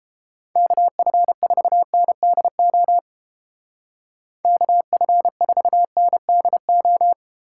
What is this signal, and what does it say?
Morse code, heard as a keyed tone, about 29 wpm, 700 Hz. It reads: KF4NDO KF4NDO